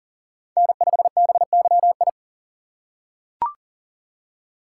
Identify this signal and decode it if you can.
Morse code, heard as a keyed tone, about 39 wpm, 700 Hz. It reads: N5BYI
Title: USA Call Signs - 5 Characters - 4x Character Spacing - 40wpm